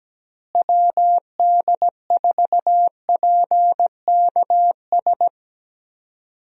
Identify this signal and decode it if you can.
Morse code, heard as a keyed tone, about 17 wpm, 700 Hz. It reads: WD4PKS
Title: USA Call Signs - 6 Characters - Repeated 3x - 17wpm